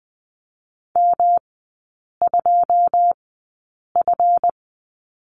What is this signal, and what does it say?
Morse code, heard as a keyed tone, about 20 wpm, spaced as for 8 wpm, 700 Hz. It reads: M2F
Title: 138 Lesson 23e - Letter f - Three Characters 20wpm